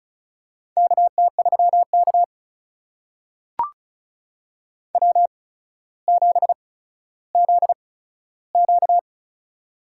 Morse code, heard as a keyed tone, about 36 wpm, spaced as for 15 wpm, 700 Hz. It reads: KT3K W7ZQ
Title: USA Call Signs - 4 Characters - 8x Character Spacing - 35wpm